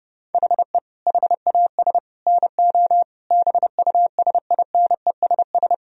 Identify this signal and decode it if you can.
Morse code, heard as a keyed tone, about 30 wpm, 700 Hz. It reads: HE HAS NO BUSINESS
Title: Sentences from Top 1000 Words - Review Part 4 - 30wpm